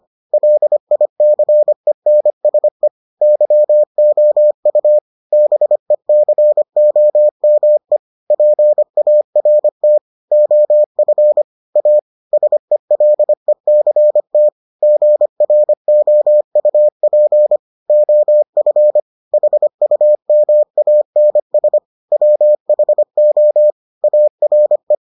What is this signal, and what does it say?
Morse code, heard as a keyed tone, about 25 wpm, 600 Hz. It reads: LICENSE YOU BECOME PART OF A SELECT GROUP OF HUMANS WHO ARE